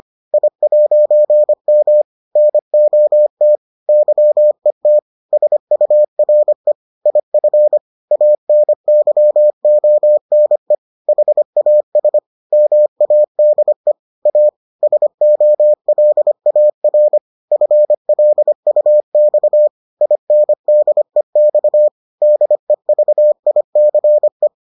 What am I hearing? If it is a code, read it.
Morse code, heard as a keyed tone, about 25 wpm, 600 Hz. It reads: I'M NOT YET SURE IF ANYONE HAS MADE A SOLAR FLUX INDEX DEVICE